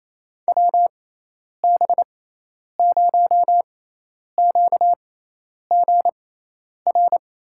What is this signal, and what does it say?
Morse code, heard as a keyed tone, about 28 wpm, spaced as for 10 wpm, 700 Hz. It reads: WB0QGR
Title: USA Call Signs - 6 Characters - 6x Character Spacing - 28wpm